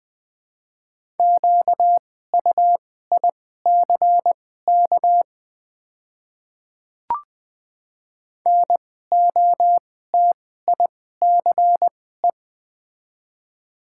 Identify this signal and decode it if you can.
Morse code, heard as a keyed tone, about 20 wpm, spaced as for 15 wpm, 700 Hz. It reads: QUICK NOTICE
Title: 2x Character Spacing - Top 500 Words - 20wpm